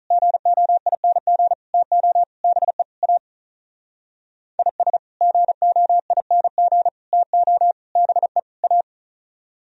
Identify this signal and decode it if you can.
Morse code, heard as a keyed tone, about 44 wpm, 700 Hz. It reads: GOING TO BE A IS GOING TO BE A